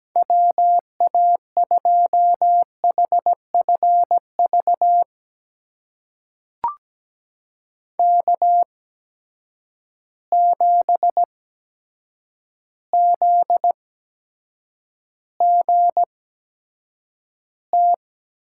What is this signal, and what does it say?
Morse code, heard as a keyed tone, about 17 wpm, spaced as for 8 wpm, 700 Hz. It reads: WA2HFV K7ZGT
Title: USA Call Signs - 8x Character Spacing - 17wpm